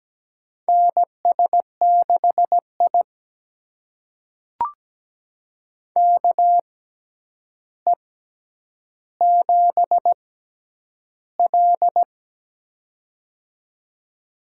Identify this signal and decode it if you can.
Morse code, heard as a keyed tone, about 17 wpm, spaced as for 9 wpm, 700 Hz. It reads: NS6I KE7L
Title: USA Call Signs - 6x Character Spacing - 17wpm